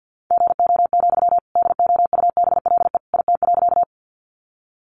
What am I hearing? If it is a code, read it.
Morse code, heard as a keyed tone, about 50 wpm, 700 Hz. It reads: GO, DOUBLE IT.
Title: Sentences from Top 1400 Words 50wpm